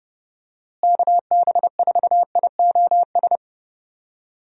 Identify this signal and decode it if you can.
Morse code, heard as a keyed tone, about 30 wpm, 700 Hz. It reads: KB4IOS
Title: USA Call Signs - 6 Characters - Repeated 2x - 30wpm